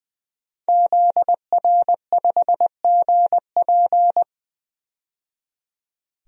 Morse code, heard as a keyed tone, about 20 wpm, 700 Hz. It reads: ZR5GP
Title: DXCC-100 Call Signs - 5 Characters - 20wpm